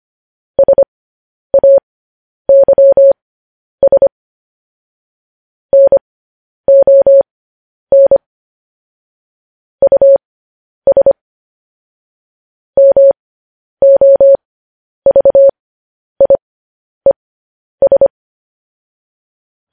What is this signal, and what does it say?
Morse code, heard as a keyed tone, about 25 wpm, spaced as for 10 wpm, 550 Hz. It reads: SAYS NON US MOVIES
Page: news-headlines-in-morse-code-at-10-wpm